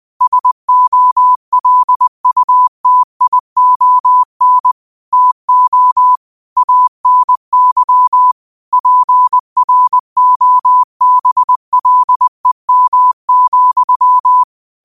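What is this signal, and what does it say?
Morse code, heard as a keyed tone, about 20 wpm, 1000 Hz. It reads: SOLUTION TO ANY PROBLEM,
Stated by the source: Morse code quote of the day at 20 wpm.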